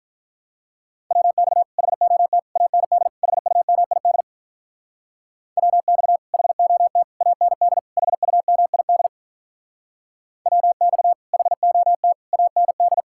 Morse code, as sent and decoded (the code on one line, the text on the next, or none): .-- -..- / .... --- - / .- -. -.. / .... ..- -- .. -.. / .-- -..- / .... --- - / .- -. -.. / .... ..- -- .. -.. / .-- -..- / .... --- - / .- -. -..
WX HOT AND HUMID WX HOT AND HUMID WX HOT AND